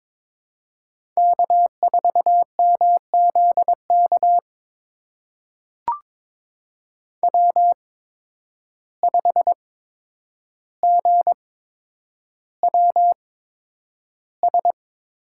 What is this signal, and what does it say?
Morse code, heard as a keyed tone, about 23 wpm, spaced as for 10 wpm, 700 Hz. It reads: K4MZK W5GWS
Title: USA Call Signs - 5 Characters - 8x Character Spacing - 22wpm